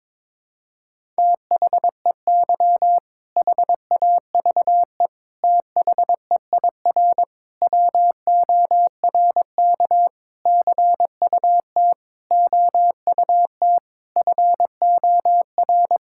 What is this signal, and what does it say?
Morse code, heard as a keyed tone, about 22 wpm, 700 Hz. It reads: THEY HAVE THEIR WORK CUT OUT FOR
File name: Sentences with 7-Word Phrases - 22wpm